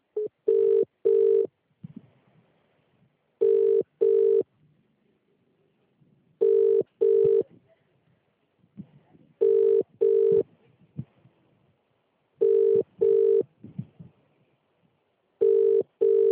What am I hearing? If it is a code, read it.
Ringback tone (UK), 400 + 450 Hz.